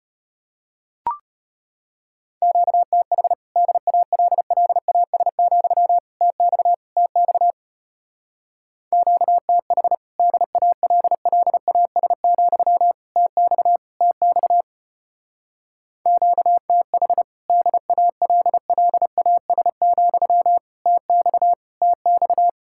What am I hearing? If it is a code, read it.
Morse code, heard as a keyed tone, about 33 wpm, 700 Hz. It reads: QTH DALLAS, TX TX QTH DALLAS, TX TX QTH DALLAS, TX TX